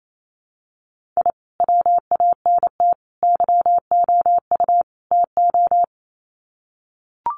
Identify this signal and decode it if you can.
Morse code, heard as a keyed tone, about 28 wpm, 700 Hz. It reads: I WANT YOU TO
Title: Sets of 4 Words 28wpm